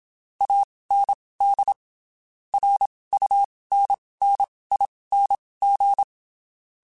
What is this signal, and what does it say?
Morse code, heard as a keyed tone, about 27 wpm, spaced as for 19 wpm, 800 Hz. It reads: AND RUNNING
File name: (spaced) / QSOs Spaced-mp3s / WD0FBT 1 @25 sp